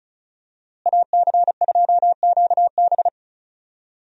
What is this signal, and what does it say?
Morse code, heard as a keyed tone, about 34 wpm, 700 Hz. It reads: AC2QB
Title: USA Call Signs - 5 Characters - Repeated 2x - 35wpm